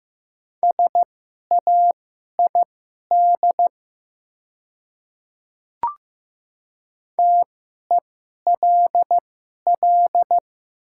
Morse code, heard as a keyed tone, about 15 wpm, spaced as for 11 wpm, 700 Hz. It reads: SAID TELL
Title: 2x Character Spacing - Top 200 Words - 15wpm